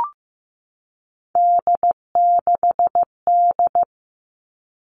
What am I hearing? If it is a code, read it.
Morse code, heard as a keyed tone, about 15 wpm, 700 Hz. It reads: D6D
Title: All Cut Numbers - 3-Digit 15wpm